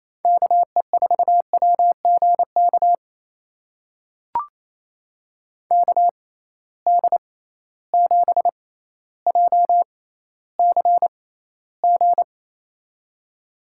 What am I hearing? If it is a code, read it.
Morse code, heard as a keyed tone, about 28 wpm, spaced as for 14 wpm, 700 Hz. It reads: KE4WGK KD7JCG
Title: USA Call Signs - 6 Characters - 6x Character Spacing - 28wpm